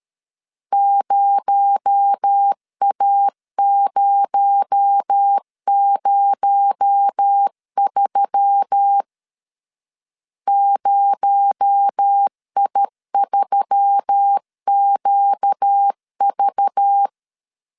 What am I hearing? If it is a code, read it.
Morse code, heard as a keyed tone, about 13 wpm, 800 Hz. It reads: 0A003 0I3QV
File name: LEZIONI / 6 PKX0 / 6 PKX0 12wpm 2231